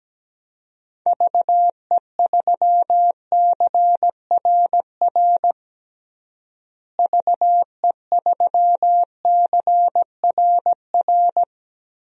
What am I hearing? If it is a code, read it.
Morse code, heard as a keyed tone, about 17 wpm, 700 Hz. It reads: VE3CRR VE3CRR
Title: DXCC-100 Call Signs - 6 Characters - Repeated 3x - 17wpm